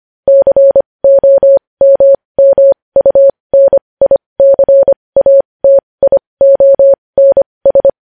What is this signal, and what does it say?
Morse code, heard as a keyed tone, about 25 wpm, spaced as for 20 wpm, 550 Hz. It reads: COMMUNICATIONS
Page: news-headlines-in-morse-code-at-20-wpm